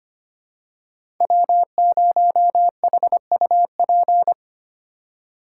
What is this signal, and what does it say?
Morse code, heard as a keyed tone, about 25 wpm, 700 Hz. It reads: W0HUP
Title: USA Call Signs - 3x Character Spacing - 25wpm